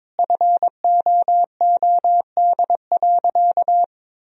Morse code, heard as a keyed tone, about 22 wpm, 700 Hz. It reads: FOOD.
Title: Sentences from Top 1000 Words - Review Part 4 - 22wpm